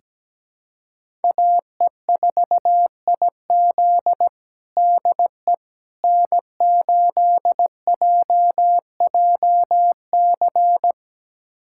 Morse code, heard as a keyed tone, about 17 wpm, 700 Hz. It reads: AE4IZ DE N8JJC
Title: USA Callsign DE Callsign - 5 Characters - 17wpm